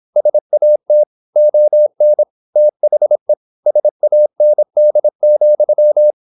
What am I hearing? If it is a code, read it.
Morse code, heard as a keyed tone, about 26 wpm, 600 Hz. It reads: SAT ON THE SAND,
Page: Never 26wpm | CW med Gnister